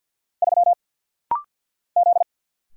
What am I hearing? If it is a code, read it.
Morse code, heard as a keyed tone, about 50 wpm, 690 Hz. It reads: 3 Z
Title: Single Letter-Number Alt - Rapid-Fire 50wpm